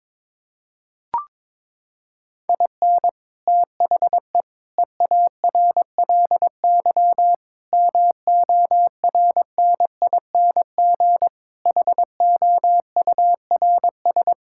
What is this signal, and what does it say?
Morse code, heard as a keyed tone, about 22 wpm, 700 Hz. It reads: IN THE EARLY MORNING HOURS
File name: practice / 22wpm / Sets of 6 Words - 22wpm